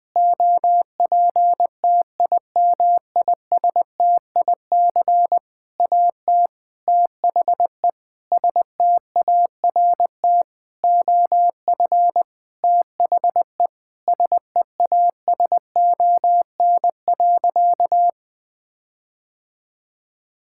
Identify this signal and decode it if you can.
Morse code, heard as a keyed tone, about 20 wpm, 700 Hz. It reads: OPTIMISTIC AT THE START OF THE SEASON.
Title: Sentences with 6-Word Phrases - 20wpm